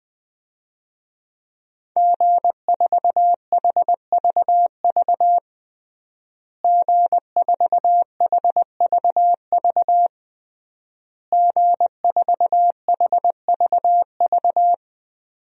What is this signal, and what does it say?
Morse code, heard as a keyed tone, about 20 wpm, 700 Hz. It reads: G4HVV G4HVV G4HVV
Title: DXCC-100 Call Signs - 5 Characters - Repeated 3x - 20wpm